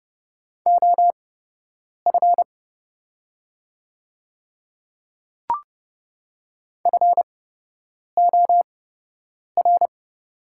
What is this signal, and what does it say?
Morse code, heard as a keyed tone, about 30 wpm, spaced as for 8 wpm, 700 Hz. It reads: OF FOR
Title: 8x Character Spacing - Top 100 Words - 30wpm